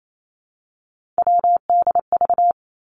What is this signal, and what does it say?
Morse code, heard as a keyed tone, about 28 wpm, 700 Hz. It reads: WDV
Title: Three Challenging Letter-Number Combinations 28wpm